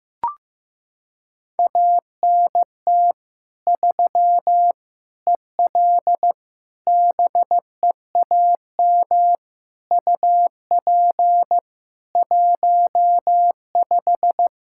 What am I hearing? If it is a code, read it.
Morse code, heard as a keyed tone, about 15 wpm, 700 Hz. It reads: ANT 3 EL BEAM UP 15
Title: K1USN SST QSO Practice 15wpm